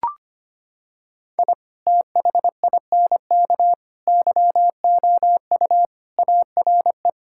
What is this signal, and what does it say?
Morse code, heard as a keyed tone, about 25 wpm, 700 Hz. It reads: I THINK YOU ARE
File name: practice / 25wpm / Sentences with 3-Word Phrases - 25wpm